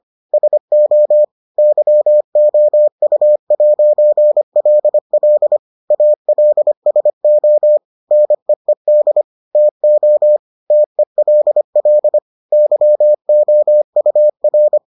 Morse code, heard as a keyed tone, about 25 wpm, 600 Hz. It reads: SO YOU'LL ALSO NEED TO TELL YOUR